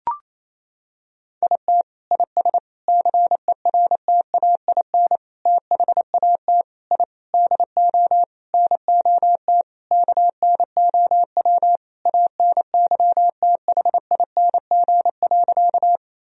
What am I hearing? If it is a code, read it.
Morse code, heard as a keyed tone, about 27 wpm, 700 Hz. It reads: IT IS CERTAIN THAT I DO NOT KNOW ANYTHING.
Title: Sentences from Top 1000 Words - Review Part 2 - 28wpm